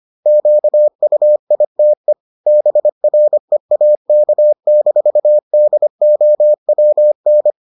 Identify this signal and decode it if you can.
Morse code, heard as a keyed tone, about 25 wpm, 600 Hz. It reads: QUITE BREAK-DOWN